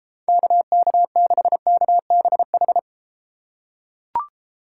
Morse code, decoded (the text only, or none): KK6KBH